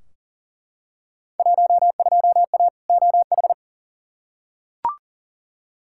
Morse code, heard as a keyed tone, about 40 wpm, 700 Hz. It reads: 12A OH
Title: Field Day Exchanges Encore - 40wpm